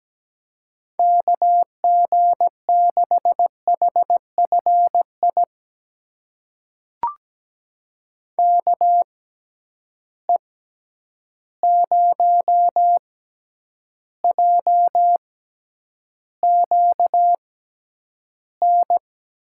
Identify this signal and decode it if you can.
Morse code, heard as a keyed tone, about 17 wpm, spaced as for 9 wpm, 700 Hz. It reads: KG6HFI KE0JQN
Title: USA Call Signs - 6 Characters - 6x Character Spacing - 17wpm